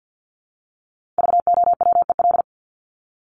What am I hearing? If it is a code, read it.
Morse code, heard as a keyed tone, about 50 wpm, 700 Hz. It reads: VOWEL
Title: Top 500 Words Repeated 50wpm